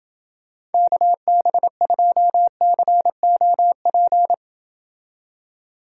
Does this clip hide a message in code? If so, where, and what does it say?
Yes: Morse code from 0.7 to 4.3 s, heard as a keyed tone, about 28 wpm, 700 Hz. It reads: KB2COP